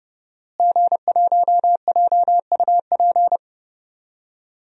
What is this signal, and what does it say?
Morse code, heard as a keyed tone, about 30 wpm, 700 Hz. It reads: G1JUP
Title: DXCC-100 Call Signs - 5 Characters - Repeated 3x - 30wpm